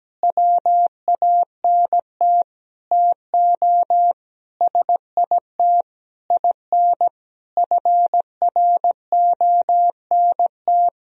Morse code, heard as a keyed tone, about 17 wpm, 700 Hz. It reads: WANT TO SIT IN FRONT
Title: Sentences from Top 1000 Words - Review Part 2 - 17wpm